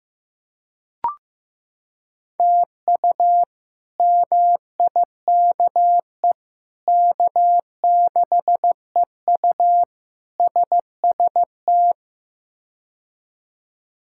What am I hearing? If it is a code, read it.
Morse code, heard as a keyed tone, about 15 wpm, 700 Hz. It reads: TU MIKE K6EU SST
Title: K1USN SST QSO Practice - 15wpm